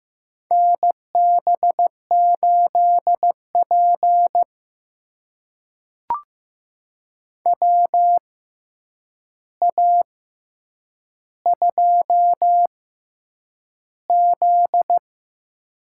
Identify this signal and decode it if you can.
Morse code, heard as a keyed tone, about 15 wpm, spaced as for 8 wpm, 700 Hz. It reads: NB8P WA2Z
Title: USA Call Signs - 4 Characters - 6x Character Spacing - 15wpm